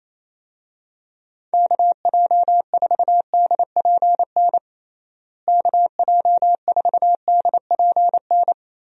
Morse code, heard as a keyed tone, about 28 wpm, 700 Hz. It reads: KJ4DPN KJ4DPN
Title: USA Call Signs - 6 Characters - Repeated 3x - 28wpm